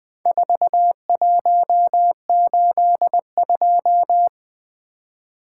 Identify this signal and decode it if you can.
Morse code, heard as a keyed tone, about 23 wpm, 700 Hz. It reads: 4182
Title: Speed-Racing - Four Letter-Number Combinations - 20wpm